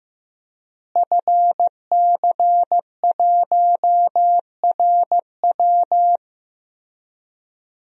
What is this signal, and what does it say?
Morse code, heard as a keyed tone, about 15 wpm, 700 Hz. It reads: FC1RW
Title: DXCC-100 Call Signs - 5 Characters - Repeated 3x - 15wpm